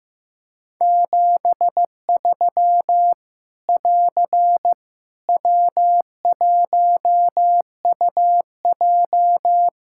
Morse code, heard as a keyed tone, about 15 wpm, 700 Hz. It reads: 73 <AR> W1UJ
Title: Practice QSO - Exchange 4 - Ending - 15wpm